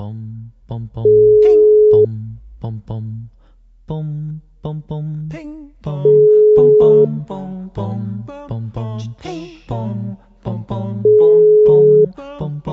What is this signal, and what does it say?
Ringback tone (Europe, 425 Hz), with speech.